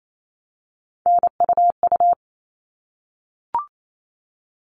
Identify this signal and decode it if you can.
Morse code, heard as a keyed tone, about 28 wpm, 700 Hz. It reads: NUU
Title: All Cut Numbers - 3-Digit 28wpm